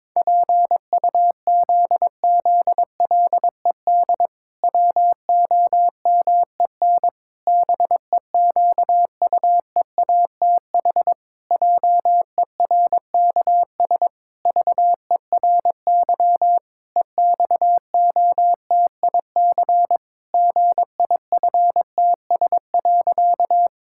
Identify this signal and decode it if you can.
Morse code, heard as a keyed tone, about 22 wpm, 700 Hz. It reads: PUZZLED WOMEN BEQUEATH JERKS VERY EXOTIC GIFTS.
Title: Pangram Challenge v2 - 22wpm